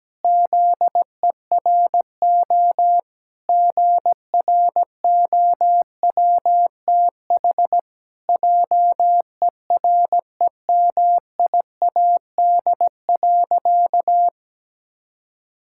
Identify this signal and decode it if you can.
Morse code, heard as a keyed tone, about 17 wpm, 700 Hz. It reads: ZERO GROWTH JEREMIAD.
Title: Pangram Challenge v2 - 17wpm